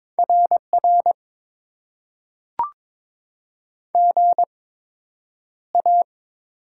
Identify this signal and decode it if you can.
Morse code, heard as a keyed tone, about 22 wpm, spaced as for 9 wpm, 700 Hz. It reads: RR GA
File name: Top 90 QSO Elements - 8x Character Spacing - 22wpm